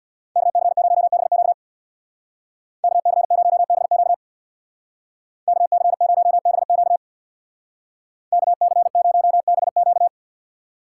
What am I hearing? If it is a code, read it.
Morse code, heard as a keyed tone, about 63 wpm, 700 Hz. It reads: KC0BQ KC0BQ KC0BQ KC0BQ